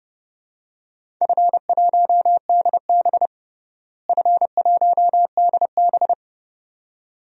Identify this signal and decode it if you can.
Morse code, heard as a keyed tone, about 31 wpm, 700 Hz. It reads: F1DB F1DB
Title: DXCC-100 Call Signs - 4 Characters - Repeated 2x - 30wpm